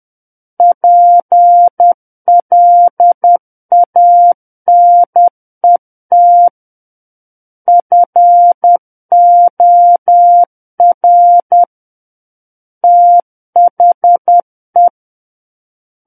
Morse code, heard as a keyed tone, about 10 wpm, 700 Hz. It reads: PLANET FOR THE
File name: Book One / 01-Chapter 1 - 10 WPM